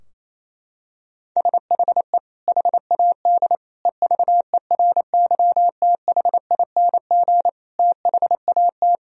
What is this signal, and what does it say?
Morse code, heard as a keyed tone, about 28 wpm, 700 Hz. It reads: SHE HAD EVERYTHING THAT
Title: Sentences from Top 1000 Words - Review Part 4 - 28wpm